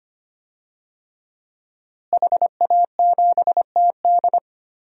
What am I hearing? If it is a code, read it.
Morse code, heard as a keyed tone, about 25 wpm, 700 Hz. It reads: HA7TD
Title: DXCC-100 Call Signs - 5 Characters - Repeated 3x - 25wpm